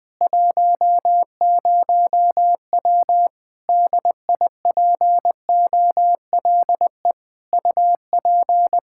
Morse code, heard as a keyed tone, about 20 wpm, 700 Hz. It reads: 10W DIPOLE UP
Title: Practice QSO - Exchange 2 - Info - 20wpm